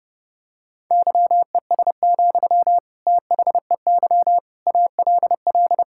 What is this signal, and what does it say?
Morse code, heard as a keyed tone, about 28 wpm, 700 Hz. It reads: YES, THEY ALL
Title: Sentences from Top 1600 Words 30wpm